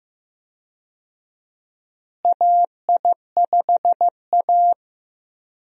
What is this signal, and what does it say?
Morse code, heard as a keyed tone, about 15 wpm, 700 Hz. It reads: AI5A
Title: USA Call Signs - 4 Characters - Repeated 2x - 15wpm